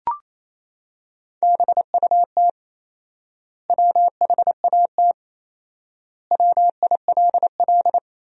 Morse code, heard as a keyed tone, about 28 wpm, 700 Hz. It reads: BUT WHAT WILL
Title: 4x Word-Spacing - Sentences from Top 100 Words - 28wpm